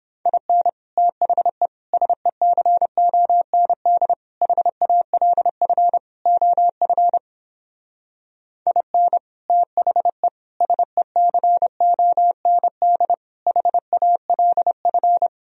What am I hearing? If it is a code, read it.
Morse code, heard as a keyed tone, about 29 wpm, 700 Hz. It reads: IN THE SECOND HALF OF IN THE SECOND HALF